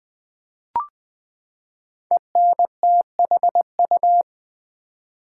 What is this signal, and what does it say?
Morse code, heard as a keyed tone, about 20 wpm, 700 Hz. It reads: ENTHU
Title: Build A Word - 5 Letters At A Time - 10-Letter Words - 20wpm